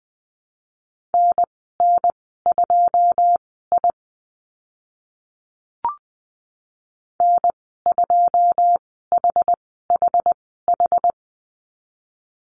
Morse code, heard as a keyed tone, about 20 wpm, spaced as for 15 wpm, 700 Hz. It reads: NN2I N2HHH
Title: 088 Lesson 15g - Number 5 - Call Signs 20wpm